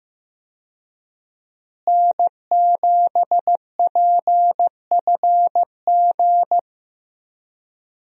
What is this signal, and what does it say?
Morse code, heard as a keyed tone, about 15 wpm, 700 Hz. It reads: N7PFG
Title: USA Call Signs - 5 Characters - Repeated 2x - 15wpm